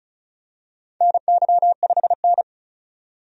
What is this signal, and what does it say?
Morse code, heard as a keyed tone, about 34 wpm, 700 Hz. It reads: NY5N
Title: USA Call Signs - 4 Characters - 35wpm